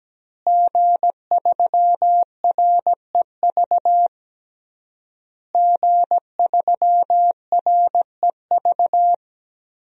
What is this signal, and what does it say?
Morse code, heard as a keyed tone, about 17 wpm, 700 Hz. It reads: G3REV G3REV
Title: DXCC-100 Call Signs - 5 Characters - Repeated 3x - 17wpm